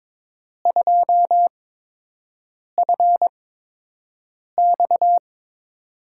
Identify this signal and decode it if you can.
Morse code, heard as a keyed tone, about 22 wpm, spaced as for 6 wpm, 700 Hz. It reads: 2FX
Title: USA Call Signs - 5 Characters - 8x Character Spacing - 22wpm